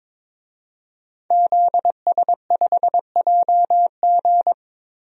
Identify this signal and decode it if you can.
Morse code, heard as a keyed tone, about 22 wpm, 700 Hz. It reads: ZS5JG
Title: DXCC-100 Call Signs - 5 Characters - Repeated 3x - 22wpm